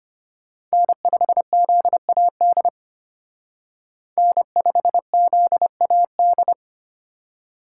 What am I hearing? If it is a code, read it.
Morse code, heard as a keyed tone, about 30 wpm, 700 Hz. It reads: N5ZAD N5ZAD
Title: Speed Racing - USA 1x3 Call Signs 25wpm